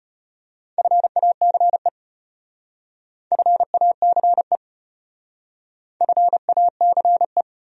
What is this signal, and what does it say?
Morse code, heard as a keyed tone, about 32 wpm, 700 Hz. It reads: FACE FACE FACE